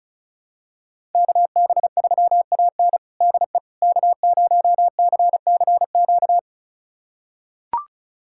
Morse code, heard as a keyed tone, about 34 wpm, 700 Hz. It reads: KB3AN DE K0CCQ
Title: USA Callsign DE Callsign - 5 Characters - 35wpm